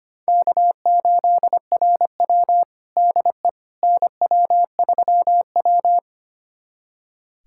Morse code, heard as a keyed tone, about 25 wpm, 700 Hz. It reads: K8RW DE NW3W
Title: USA Callsign DE Callsign - 4 Characters - 25wpm